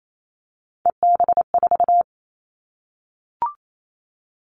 Morse code, heard as a keyed tone, about 28 wpm, 700 Hz. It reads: EB4
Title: Callsign Prefixes for Top 25 DXCC - Repeated 3x - 28wpm